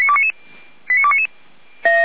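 Five-tone selective call (ZVEI), heard as quick runs of five tones, tones of about 70 ms. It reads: 89299
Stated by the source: Five-tone ZVEI selective call received on SDR, from the feuerwehr-orlamuende.org fire brigade audio archive.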